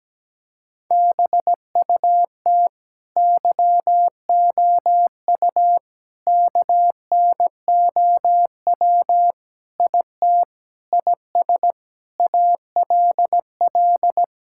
Morse code, heard as a keyed tone, about 17 wpm, 700 Hz. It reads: BUT YOU KNOW IT IS ALL
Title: Sentences from Top 1600 Words 17wpm